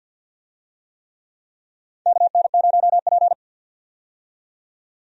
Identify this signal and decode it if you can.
Morse code, heard as a keyed tone, about 52 wpm, 700 Hz. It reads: KN0P
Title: USA Call Signs - 4 Characters - 50wpm